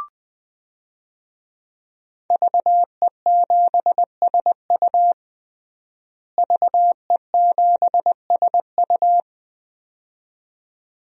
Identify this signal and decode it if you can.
Morse code, heard as a keyed tone, about 21 wpm, 700 Hz. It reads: VE7SU VE7SU
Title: DXCC-100 Call Signs - 5 Characters - Repeated 2x - 20wpm